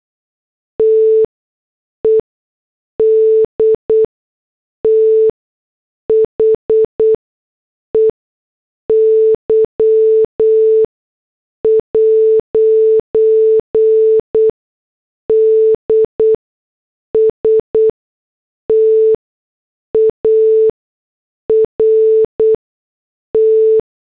Morse code, heard as a keyed tone, about 8 wpm, spaced as for 6 wpm, 440 Hz. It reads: TED THEY'D START
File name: morse / headlines